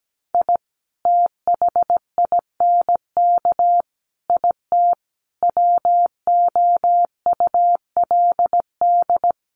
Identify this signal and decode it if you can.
Morse code, heard as a keyed tone, about 17 wpm, 700 Hz. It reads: I THINK IT WOULD
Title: Sets of 5 Words 17wpm